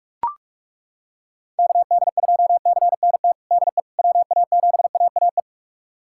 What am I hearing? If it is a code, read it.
Morse code, heard as a keyed tone, about 41 wpm, 700 Hz. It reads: KD2CNT DE WA7AAE
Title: USA Callsign DE Callsign - 6 Characters - 45wpm